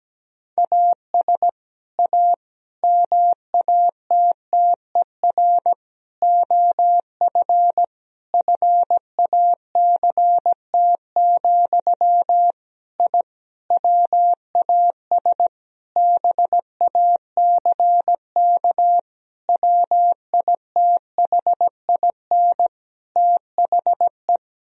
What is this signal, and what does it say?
Morse code, heard as a keyed tone, about 17 wpm, 700 Hz. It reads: AS A MATTER OF FACT, I WAS BACK WITHIN THE
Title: Sentences from Top 1000 Words - Review Part 4 - 17wpm